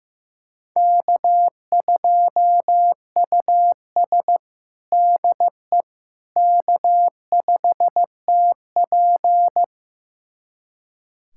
Morse code, heard as a keyed tone, about 15 wpm, 700 Hz. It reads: K2US DE K5TP
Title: USA Callsign DE Callsign - 4 Characters - 15wpm